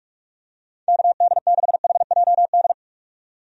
Morse code, heard as a keyed tone, about 46 wpm, 700 Hz. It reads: KD6HJD
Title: USA Call Signs - 6 Characters - 45wpm